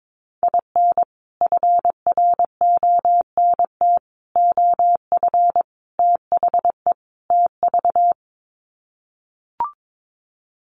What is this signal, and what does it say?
Morse code, heard as a keyed tone, about 22 wpm, 700 Hz. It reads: IN FRONT OF THE TV
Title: Sets of 5 Words Encore2 22wpm